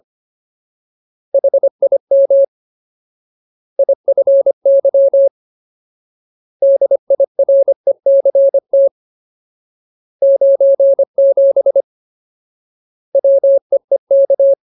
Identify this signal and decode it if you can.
Morse code, heard as a keyed tone, about 25 wpm, 550 Hz. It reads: HIM IFY DIRECT 97 WEEK
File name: Ultimate Word List / 25WPM Ultimate